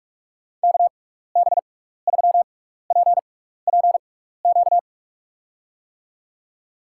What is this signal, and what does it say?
Morse code, heard as a keyed tone, about 46 wpm, spaced as for 15 wpm, 700 Hz. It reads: KB3PPQ
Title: USA Call Signs - 6 Characters - 6x Character Spacing - 45wpm